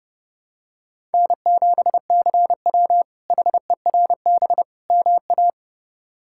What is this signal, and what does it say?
Morse code, heard as a keyed tone, about 30 wpm, 700 Hz. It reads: N7CW HERB MA
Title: K1USN SST QSO Practice - 30wpm